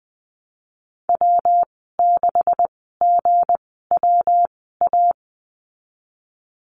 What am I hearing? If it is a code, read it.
Morse code, heard as a keyed tone, about 20 wpm, spaced as for 15 wpm, 700 Hz. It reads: W6GWA
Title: 179 Lesson 29g - Slash - Call Signs 20wpm